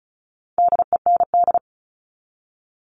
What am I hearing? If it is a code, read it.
Morse code, heard as a keyed tone, about 34 wpm, 700 Hz. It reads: DEND
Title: All Cut Numbers - 4-Digit 35wpm